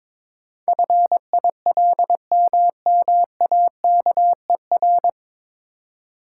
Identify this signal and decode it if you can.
Morse code, heard as a keyed tone, about 22 wpm, 700 Hz. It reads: FILMMAKER
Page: Morse Code Ninja practice 22wpm